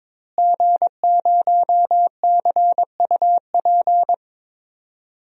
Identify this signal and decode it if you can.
Morse code, heard as a keyed tone, about 22 wpm, 700 Hz. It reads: G0CUP
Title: DXCC-100 Call Signs - 5 Characters - Repeated 3x - 22wpm